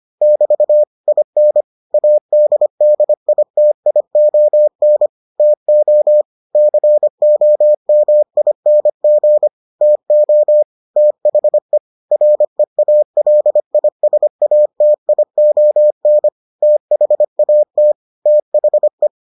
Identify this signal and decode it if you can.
Morse code, heard as a keyed tone, about 25 wpm, 600 Hz. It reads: <BT> IN ADDITION TO COMING TO THE REALISATION THAT THE